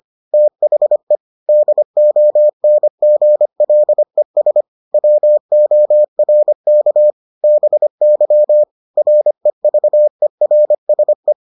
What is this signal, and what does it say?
Morse code, heard as a keyed tone, about 25 wpm, 600 Hz. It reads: THE DONGLES WORK BY REVERSE